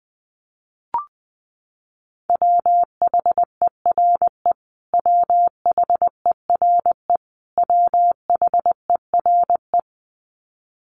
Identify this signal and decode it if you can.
Morse code, heard as a keyed tone, about 20 wpm, 700 Hz. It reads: WHERE WHERE WHERE
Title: Top 300 Words Repeated 20wpm